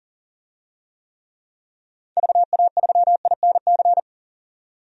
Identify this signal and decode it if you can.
Morse code, heard as a keyed tone, about 40 wpm, 700 Hz. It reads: VA3INC
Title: DXCC-100 Call Signs - 6 Characters - 40wpm